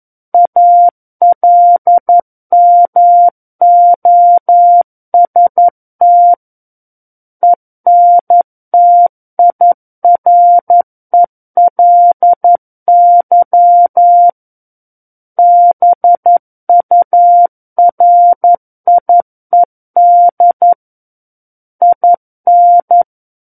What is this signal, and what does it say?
Morse code, heard as a keyed tone, about 11 wpm, 700 Hz. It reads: ALMOST ENTIRELY BURIED IN
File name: 11 WPM